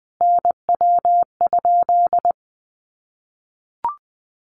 Morse code, heard as a keyed tone, about 20 wpm, 700 Hz. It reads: NW?
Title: Common QSO Phrases 20wpm